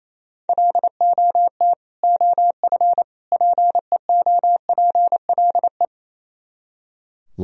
Morse code, heard as a keyed tone, about 34 wpm, 700 Hz. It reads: LOT OF PEOPLE
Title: Speed-Racing - Sets of 3 Words 28wpm